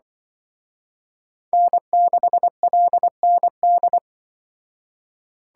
Morse code, heard as a keyed tone, about 24 wpm, 700 Hz. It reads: N6LND
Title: Speed Racing - USA 1x3 Call Signs 20wpm